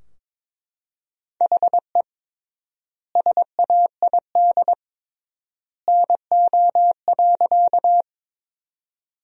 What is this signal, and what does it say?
Morse code, heard as a keyed tone, about 22 wpm, 700 Hz. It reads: HE SAID NO.
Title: 3x Word-Spacing - Sentences from Top 100 Words - 22wpm